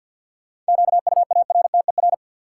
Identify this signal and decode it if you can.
Morse code, heard as a keyed tone, about 50 wpm, 700 Hz. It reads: QUARTER
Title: Sentences from Top 1300 Words - New Word Drill - 50wpm